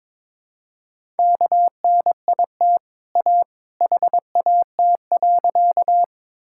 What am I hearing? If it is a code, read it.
Morse code, heard as a keyed tone, about 22 wpm, 700 Hz. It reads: KNIT A HAT.
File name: practice / 22wpm / Easy Sentences with Hints - 22wpm